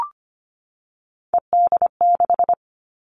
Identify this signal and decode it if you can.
Morse code, heard as a keyed tone, about 25 wpm, 700 Hz. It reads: ED6
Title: Callsign Prefixes for Top 100 DXCC - 25wpm